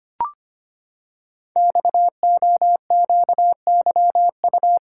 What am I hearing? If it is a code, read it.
Morse code, heard as a keyed tone, about 25 wpm, 700 Hz. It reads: XOQYU
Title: Code Groups - 5 Letters - 25wpm